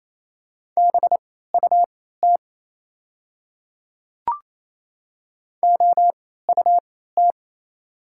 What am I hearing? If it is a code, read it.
Morse code, heard as a keyed tone, about 28 wpm, spaced as for 16 wpm, 690 Hz. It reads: BUT OUT
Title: 3x Character Spacing - Top 500 Words - 28wpm